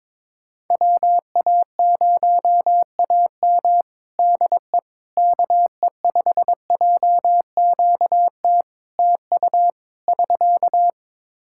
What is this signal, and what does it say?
Morse code, heard as a keyed tone, about 22 wpm, 700 Hz. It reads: WA0AM DE KE5JQT TU <SK>
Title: Practice QSO - Exchange 4 - Ending - 22wpm